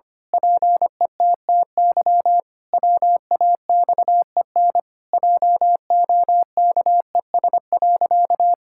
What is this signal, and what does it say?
Morse code, heard as a keyed tone, about 25 wpm, 700 Hz. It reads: PETTY WAXEN JOKES.
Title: Pangram Challenge v2 - 25wpm